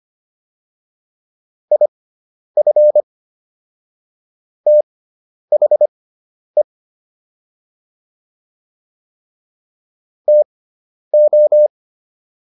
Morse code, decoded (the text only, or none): IF THE TO